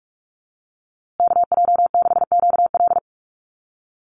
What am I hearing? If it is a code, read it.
Morse code, heard as a keyed tone, about 44 wpm, 700 Hz. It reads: KJ6QL
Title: USA Call Signs Repeated 3x 45wpm